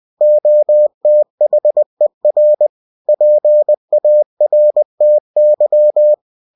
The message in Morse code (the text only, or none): OTHER PARTY